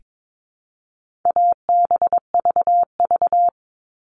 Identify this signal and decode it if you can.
Morse code, heard as a keyed tone, about 22 wpm, 700 Hz. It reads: ABVV
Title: All Cut Numbers - 4-Digit 22wpm